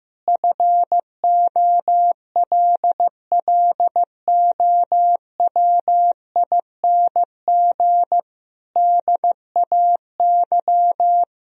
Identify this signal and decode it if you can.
Morse code, heard as a keyed tone, about 15 wpm, 700 Hz. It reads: FOLLOWING DAY
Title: Sentences from Top 1000 Words - Review Part 4 - 15wpm